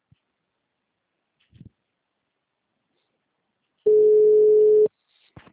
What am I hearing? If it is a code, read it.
Ringback tone (Europe, 425 Hz).